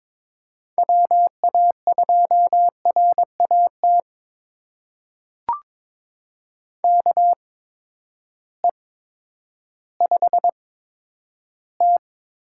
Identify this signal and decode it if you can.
Morse code, heard as a keyed tone, about 22 wpm, spaced as for 11 wpm, 700 Hz. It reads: WA2RAT KE5T
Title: USA Call Signs - 6 Characters - 8x Character Spacing - 22wpm